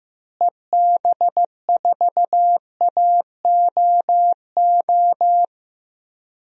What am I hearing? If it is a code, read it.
Morse code, heard as a keyed tone, about 15 wpm, 700 Hz. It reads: EB4AOO
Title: DXCC-100 Call Signs - 6 Characters - Repeated 3x - 15wpm